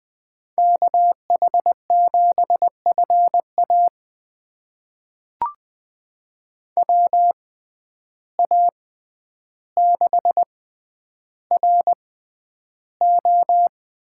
Morse code, heard as a keyed tone, about 20 wpm, spaced as for 10 wpm, 700 Hz. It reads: KH7FA WA6RO
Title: USA Call Signs - 5 Characters - 6x Character Spacing - 20wpm